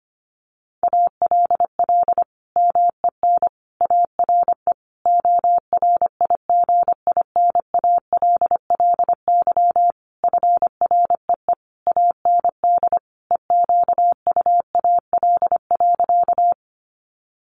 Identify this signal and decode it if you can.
Morse code, heard as a keyed tone, about 25 wpm, 700 Hz. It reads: ALL MEN ARE ORIGINALLY FREE AND EQUAL.
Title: Sentences from Top 1500 Words 25wpm